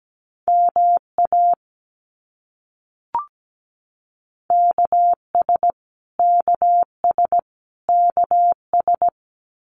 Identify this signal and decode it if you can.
Morse code, heard as a keyed tone, about 17 wpm, 700 Hz. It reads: MA KS KS KS
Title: US States Repeated 3x 17wpm